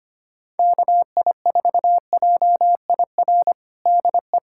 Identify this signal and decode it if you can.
Morse code, heard as a keyed tone, about 25 wpm, 700 Hz. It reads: KI4JIR DE
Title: USA Callsign DE Callsign - 6 Characters - 25wpm